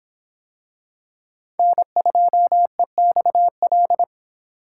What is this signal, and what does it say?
Morse code, heard as a keyed tone, about 26 wpm, 700 Hz. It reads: N2EXL